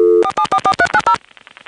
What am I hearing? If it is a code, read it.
DTMF, heard as telephone touch tones, tones of about 70 ms. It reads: 4744390